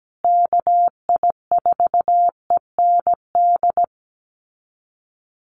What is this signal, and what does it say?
Morse code, heard as a keyed tone, about 17 wpm, 700 Hz. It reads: KI4END
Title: USA Call Signs Repeated 3x 17wpm